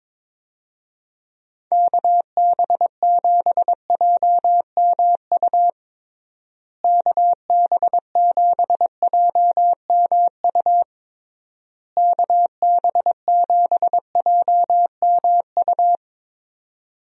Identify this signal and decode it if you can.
Morse code, heard as a keyed tone, about 22 wpm, 700 Hz. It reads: KB7JMU KB7JMU KB7JMU
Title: USA Call Signs - 6 Characters - Repeated 3x - 22wpm